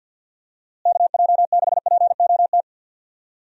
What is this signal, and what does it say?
Morse code, heard as a keyed tone, about 51 wpm, 700 Hz. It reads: KJ6POT
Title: USA Call Signs - 6 Characters - Repeated 2x - 50wpm